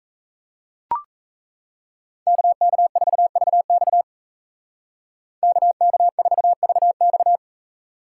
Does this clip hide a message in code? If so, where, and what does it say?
Yes: Morse code from 2.3 to 7.4 s, heard as a keyed tone, about 36 wpm, 700 Hz. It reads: KK4VX KK4VX